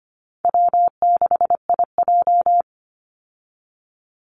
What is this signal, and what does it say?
Morse code, heard as a keyed tone, about 24 wpm, 700 Hz. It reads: W6IJ
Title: Easy Call Signs - 25wpm